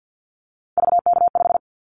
Morse code, heard as a keyed tone, about 50 wpm, 700 Hz. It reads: VK5
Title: Callsign Prefixes for Top 50 DXCC - 50wpm